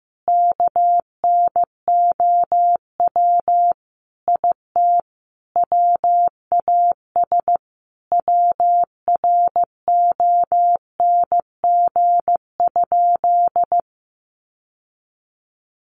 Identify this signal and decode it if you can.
Morse code, heard as a keyed tone, about 15 wpm, 700 Hz. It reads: KNOW IT WAS WRONG?
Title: Sentences from Top 1100 Words 15wpm